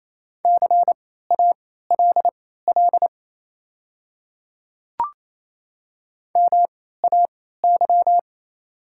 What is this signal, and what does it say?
Morse code, heard as a keyed tone, about 28 wpm, spaced as for 16 wpm, 700 Hz. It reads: CALL MAY
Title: 3x Character Spacing - Top 300 Words - 28wpm